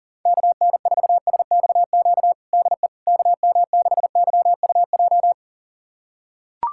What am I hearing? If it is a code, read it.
Morse code, heard as a keyed tone, about 40 wpm, 700 Hz. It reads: KN4SXQ DE KM6YUJ
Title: USA Callsign DE Callsign 40wpm